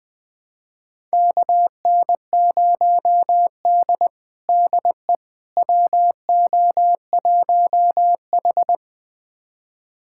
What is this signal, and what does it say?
Morse code, heard as a keyed tone, about 21 wpm, 700 Hz. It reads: KN0D DE WO1H
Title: USA Callsign DE Callsign - 4 Characters - 20wpm